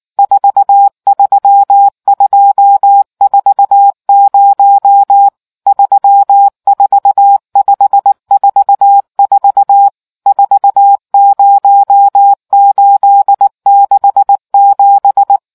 Morse code, heard as a keyed tone, about 19 wpm, 800 Hz. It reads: 43240 34544 40867